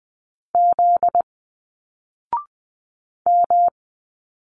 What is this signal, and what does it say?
Morse code, heard as a keyed tone, about 20 wpm, 700 Hz. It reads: Z M